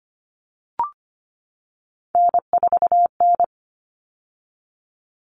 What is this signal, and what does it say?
Morse code, heard as a keyed tone, about 25 wpm, 700 Hz. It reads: N4N
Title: All Cut Numbers - 3-Digit 25wpm